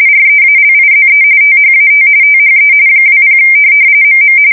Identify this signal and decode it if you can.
RTTY, about 45.45 baud. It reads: REALIZE THAT THE 20 MM
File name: fsk170 / USS-Massachusetts